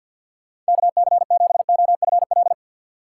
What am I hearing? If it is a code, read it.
Morse code, heard as a keyed tone, about 65 wpm, spaced as for 57 wpm, 700 Hz. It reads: KC7OFL
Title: Speed Racing - USA 2x3 Call Signs 50wpm